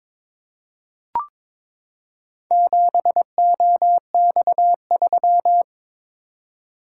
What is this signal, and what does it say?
Morse code, heard as a keyed tone, about 22 wpm, 700 Hz. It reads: 7OX3
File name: practice / 22wpm / Four Letter-Number Combinations - 22wpm